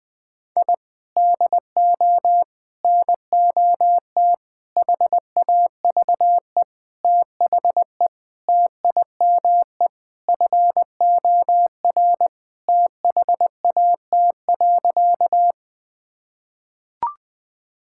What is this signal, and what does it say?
Morse code, heard as a keyed tone, about 21 wpm, 700 Hz. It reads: I DO NOT HAVE THE TIME FOR THAT.
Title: Sentences with 5-Word Phrases - 20wpm